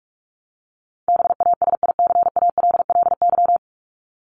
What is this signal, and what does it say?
Morse code, heard as a keyed tone, about 44 wpm, 700 Hz. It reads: BASICALLY
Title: Top 1251-1500 Words 45wpm